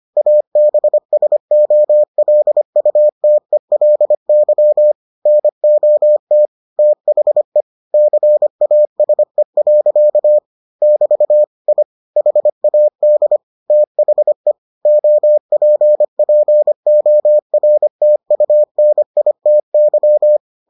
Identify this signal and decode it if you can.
Morse code, heard as a keyed tone, about 25 wpm, 600 Hz. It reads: ABSOLUTELY NOT THE CASE. <BT> I HAD THE OPPORTUNITY